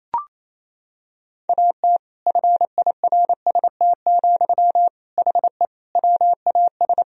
Morse code, heard as a keyed tone, about 28 wpm, 700 Hz. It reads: AT FIRST, HE WAS
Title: Sentences from Top 1000 Words - Review Part 4 - 28wpm